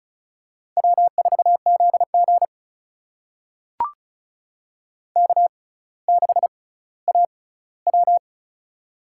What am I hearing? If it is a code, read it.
Morse code, heard as a keyed tone, about 36 wpm, spaced as for 18 wpm, 700 Hz. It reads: W4ZG K6AW
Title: USA Call Signs - 4 Characters - 6x Character Spacing - 35wpm